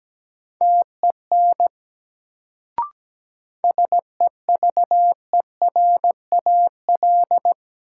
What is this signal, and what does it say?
Morse code, heard as a keyed tone, about 17 wpm, 700 Hz. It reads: TEN SEVERAL
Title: Sentences from Top 400 Words - New Word Drill - 17wpm